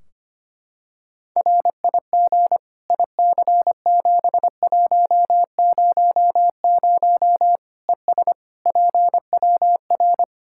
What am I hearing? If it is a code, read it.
Morse code, heard as a keyed tone, about 25 wpm, 700 Hz. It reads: RIG IC7100 ES PWR
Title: Practice QSO - Exchange 2 - Info - 25wpm